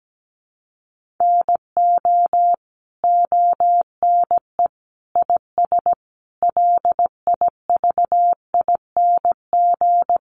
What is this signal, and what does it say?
Morse code, heard as a keyed tone, about 17 wpm, 700 Hz. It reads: NO ONE IS LIVING
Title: Sentences from Top 1100 Words 17wpm